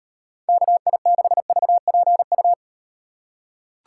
Morse code, heard as a keyed tone, about 38 wpm, 700 Hz. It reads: KI6VPU